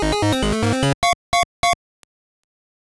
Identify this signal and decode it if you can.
There is no code.